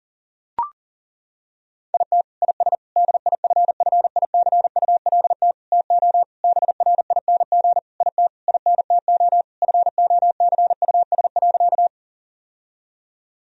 Morse code, heard as a keyed tone, about 40 wpm, 700 Hz. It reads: IT IS DIFFICULT TO BRING IT INTO FOCUS.
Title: Sentences from Top 1000 Words - Review Part 4 - 40wpm